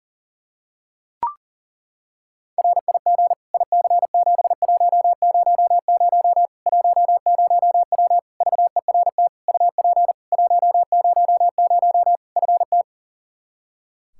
Morse code, heard as a keyed tone, about 37 wpm, 700 Hz. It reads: RIG IC7100 10W VERT UP 100 FT
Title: Practice QSO - Exchange 2 - Info - 40wpm